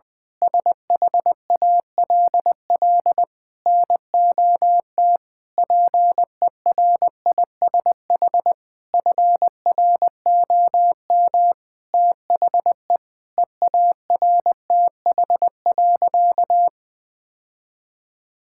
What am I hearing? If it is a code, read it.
Morse code, heard as a keyed tone, about 20 wpm, 700 Hz. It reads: SHALL NOT PERISH FROM THE EARTH.
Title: Sentences with 9-Word Phrases - 20wpm